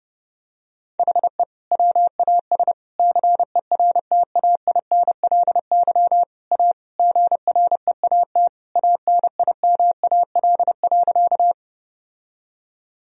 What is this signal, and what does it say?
Morse code, heard as a keyed tone, about 30 wpm, 700 Hz. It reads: HE WAS CERTAINLY A GREAT ANIMAL.
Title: Sentences from Top 1000 Words - Review Part 1 - 30wpm